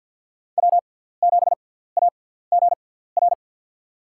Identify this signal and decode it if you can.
Morse code, heard as a keyed tone, about 53 wpm, spaced as for 17 wpm, 700 Hz. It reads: W7AGR
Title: USA Call Signs - 5 Characters - 6x Character Spacing - 50wpm